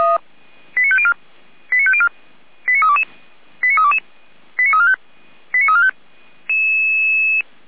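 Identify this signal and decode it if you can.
Five-tone selective call (ZVEI), heard as quick runs of five tones, tones of about 70 ms. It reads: 89504 89504 89322 89322 89345 89345
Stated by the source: ZVEI five-tone selective call received on SDR.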